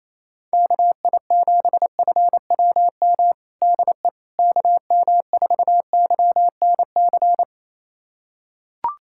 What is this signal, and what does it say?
Morse code, heard as a keyed tone, about 28 wpm, 700 Hz. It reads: KI7FWM DE KM4YNC
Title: USA Callsign DE Callsign - 6 Characters - 28wpm